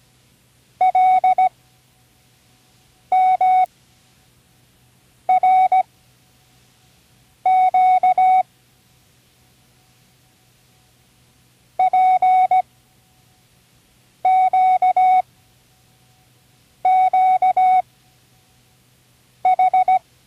Morse code, heard as a keyed tone, about 17 wpm, spaced as for 5 wpm, 750 Hz. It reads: LMRQ PQQH
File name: CWResources / Learning Morse Code (from Tune in the World) / 24 random